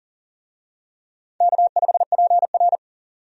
Morse code, heard as a keyed tone, about 40 wpm, 700 Hz. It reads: K5PR
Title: USA Call Signs - 4 Characters - 40wpm